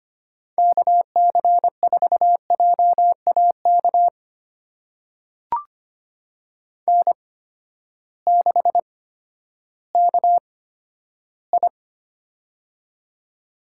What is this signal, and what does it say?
Morse code, heard as a keyed tone, about 25 wpm, spaced as for 9 wpm, 700 Hz. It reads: KC4JA KN6KI
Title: USA Call Signs - 8x Character Spacing - 25wpm